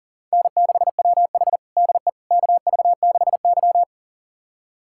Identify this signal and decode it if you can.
Morse code, heard as a keyed tone, about 40 wpm, 690 Hz. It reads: N6WH DE KV6Y
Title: USA Callsign DE Callsign - 4 Characters - 40wpm